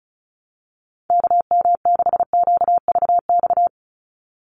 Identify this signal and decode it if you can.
Morse code, heard as a keyed tone, about 36 wpm, 700 Hz. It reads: KM6QVX
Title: USA Call Signs 35wpm